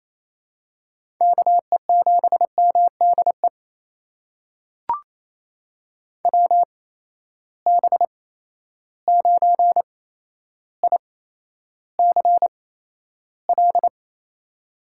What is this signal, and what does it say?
Morse code, heard as a keyed tone, about 28 wpm, spaced as for 12 wpm, 700 Hz. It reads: KE7MDE WB9ICL
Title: USA Call Signs - 6 Characters - 8x Character Spacing - 28wpm